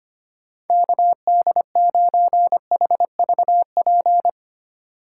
Morse code, heard as a keyed tone, about 32 wpm, 700 Hz. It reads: KD9HVP